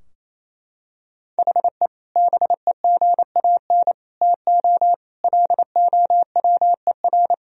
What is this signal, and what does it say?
Morse code, heard as a keyed tone, about 28 wpm, 700 Hz. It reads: HE BEGAN TO LOWER
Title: Sentences from Top 1000 Words - Review Part 4 - 28wpm